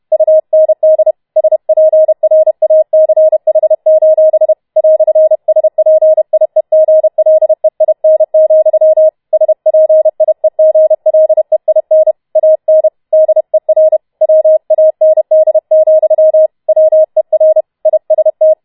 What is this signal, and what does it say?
Morse code, heard as a keyed tone, about 31 wpm, 630 Hz. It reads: UND SPRACH: "SPIEGLEIN, SPIEGLEIN AN DER WAND, WER IST